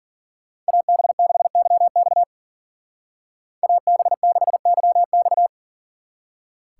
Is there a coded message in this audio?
Yes: Morse code from 0.7 to 5.5 s, heard as a keyed tone, about 53 wpm, 700 Hz. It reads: AB6YX AB6YX